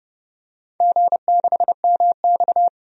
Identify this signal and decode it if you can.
Morse code, heard as a keyed tone, about 30 wpm, 700 Hz. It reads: G6MX
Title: DXCC-100 Call Signs - 4 Characters - 30wpm